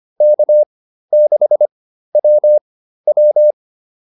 Morse code, heard as a keyed tone, about 25 wpm, spaced as for 13 wpm, 600 Hz. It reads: K6WW
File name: QSO204 13